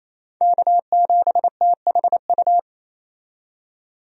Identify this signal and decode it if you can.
Morse code, heard as a keyed tone, about 28 wpm, 700 Hz. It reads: K7THU
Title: USA Call Signs - 5 Characters - 28wpm